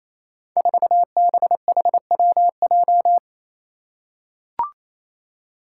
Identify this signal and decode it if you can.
Morse code, heard as a keyed tone, about 29 wpm, 700 Hz. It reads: 4BHWJ
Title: Five Challenging Letter-Number Combinations 28wpm